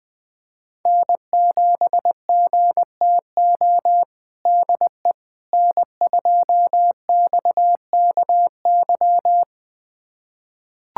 Morse code, heard as a keyed tone, about 20 wpm, 700 Hz. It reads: N7GTO DE N2XKY
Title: USA Callsign DE Callsign - 5 Characters - 20wpm